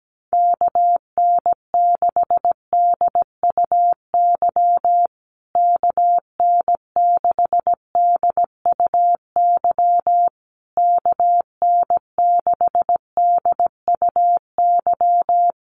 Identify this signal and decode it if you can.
Morse code, heard as a keyed tone, about 17 wpm, 700 Hz. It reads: KN6DUY KN6DUY KN6DUY
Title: USA Call Signs Repeated 3x 17wpm